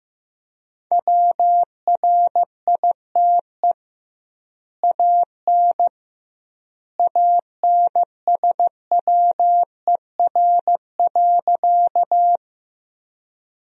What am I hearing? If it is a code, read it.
Morse code, heard as a keyed tone, about 16 wpm, 700 Hz. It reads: WRITE AN ANSWER.
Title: 2x Word-Spacing - Sentences from Top 300 Words - 15wpm